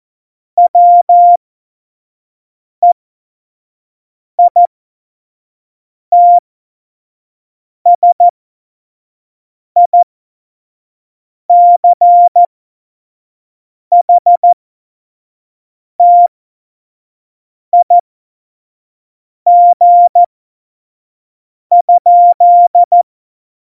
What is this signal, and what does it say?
Morse code, heard as a keyed tone, about 14 wpm, spaced as for 5 wpm, 700 Hz. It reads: WEITSICHTIG?
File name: Morsekurs / Witze / Witz 14. November 2017, 25 F70